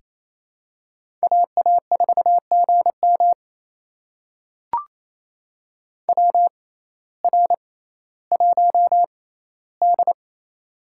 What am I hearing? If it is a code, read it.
Morse code, heard as a keyed tone, about 28 wpm, spaced as for 15 wpm, 700 Hz. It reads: AA4GM WR1D
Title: USA Call Signs - 6x Character Spacing - 28wpm